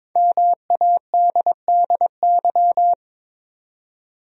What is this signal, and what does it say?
Morse code, heard as a keyed tone, about 22 wpm, 700 Hz. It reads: MADDY
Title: USA Common Names - Max 5 Letters - 22wpm